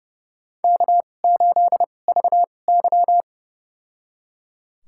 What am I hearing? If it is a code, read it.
Morse code, heard as a keyed tone, about 30 wpm, spaced as for 22 wpm, 700 Hz. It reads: K8VY
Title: USA Call Signs - 4 Characters - 2x Character Spacing - 30wpm